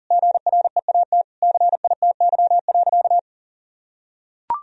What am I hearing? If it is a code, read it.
Morse code, heard as a keyed tone, about 37 wpm, 700 Hz. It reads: GREAT CITY.
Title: Sentences from Top 1000 Words - Review Part 1 - 40wpm